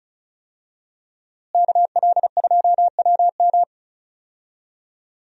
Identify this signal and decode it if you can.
Morse code, heard as a keyed tone, about 34 wpm, 700 Hz. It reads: KL2WM
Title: USA Call Signs - 5 Characters - Repeated 3x - 35wpm